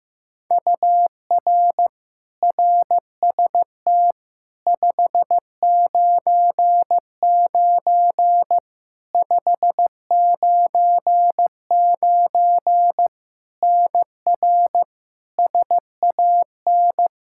Morse code, heard as a keyed tone, about 15 wpm, 700 Hz. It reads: UR RST 599 599 NR SAN
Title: K1USN SST QSO Practice 15wpm